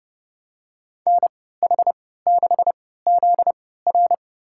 Morse code, heard as a keyed tone, about 32 wpm, spaced as for 17 wpm, 700 Hz. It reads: NH6ZR
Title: USA Call Signs - 5 Characters - 3x Character Spacing - 30wpm